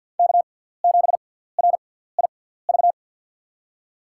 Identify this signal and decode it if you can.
Morse code, heard as a keyed tone, about 52 wpm, spaced as for 17 wpm, 700 Hz. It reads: K7RIV
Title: USA Call Signs - 6x Character Spacing - 50wpm